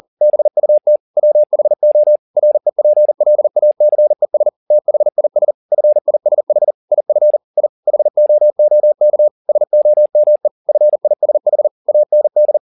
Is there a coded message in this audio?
Yes: Morse code from 0.2 to 12.6 s, heard as a keyed tone, about 40 wpm, 600 Hz. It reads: BUT WHO REPLACES THIS FISH IF I HOOK SOME FISH AND